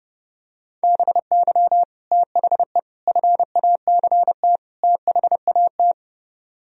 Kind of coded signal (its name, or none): Morse code